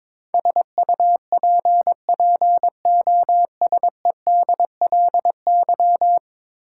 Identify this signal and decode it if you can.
Morse code, heard as a keyed tone, about 22 wpm, 700 Hz. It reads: SUPPOSEDLY
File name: practice / 22wpm / 10 Letter Words - First 3 Letters Given - 22wpm